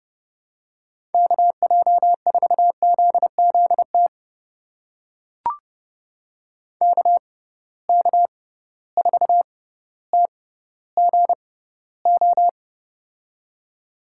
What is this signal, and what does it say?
Morse code, heard as a keyed tone, about 32 wpm, spaced as for 16 wpm, 700 Hz. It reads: KJ4ZZT KK4TGO
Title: USA Call Signs - 6 Characters - 6x Character Spacing - 30wpm